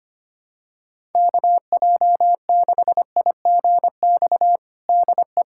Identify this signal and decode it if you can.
Morse code, heard as a keyed tone, about 25 wpm, 700 Hz. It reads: KJ6IGX DE
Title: USA Callsign DE Callsign - 6 Characters - 25wpm